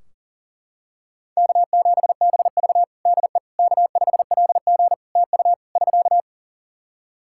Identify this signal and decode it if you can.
Morse code, heard as a keyed tone, about 40 wpm, 700 Hz. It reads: K7BV DE K5LG TU <SK>
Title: K1USN SST QSO Practice 40wpm